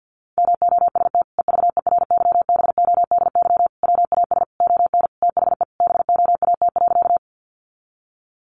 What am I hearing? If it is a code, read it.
Morse code, heard as a keyed tone, about 50 wpm, 700 Hz. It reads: MOST EVERYBODY WAS ON THE BOAT.
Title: Sentences from Top 1400 Words 50wpm